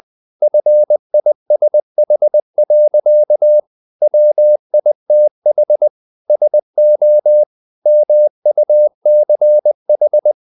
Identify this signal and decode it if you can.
Morse code, heard as a keyed tone, about 20 wpm, 600 Hz. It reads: FISH. WITH SO MUCH
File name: the old man and the sea part 2 20wpm